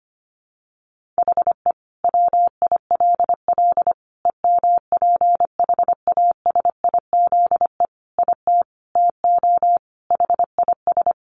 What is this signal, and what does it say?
Morse code, heard as a keyed tone, about 25 wpm, 700 Hz. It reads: HE WILL EMPHASIZE IT TO HIS
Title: Sentences from Top 1500 Words 25wpm